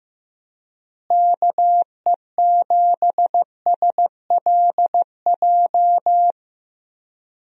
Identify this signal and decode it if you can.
Morse code, heard as a keyed tone, about 15 wpm, 700 Hz. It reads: KE7SLJ
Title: USA Call Signs - 6 Characters - Repeated 3x - 15wpm